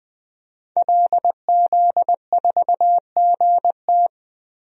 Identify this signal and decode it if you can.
Morse code, heard as a keyed tone, about 20 wpm, 700 Hz. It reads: LZ4GT
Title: DXCC-100 Call Signs - 5 Characters - 20wpm